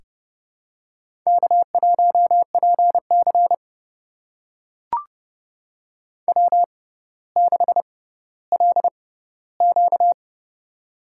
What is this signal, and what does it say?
Morse code, heard as a keyed tone, about 30 wpm, spaced as for 15 wpm, 700 Hz. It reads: K1PC W6LQ
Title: USA Call Signs - 6x Character Spacing - 30wpm